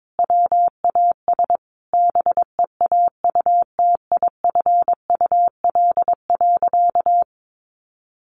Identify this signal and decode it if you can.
Morse code, heard as a keyed tone, about 22 wpm, 700 Hz. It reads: WAS BEAUTIFUL.
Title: Sentences from Top 900 Words 22wpm